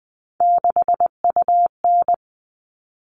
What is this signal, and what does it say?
Morse code, heard as a keyed tone, about 20 wpm, 700 Hz. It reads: 6UN
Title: All Cut Numbers - 3-Digit 20wpm